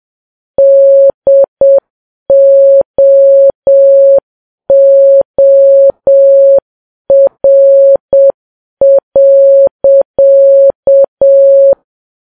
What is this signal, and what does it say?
Morse code, heard as a keyed tone, about 7 wpm, 550 Hz. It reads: DOOR.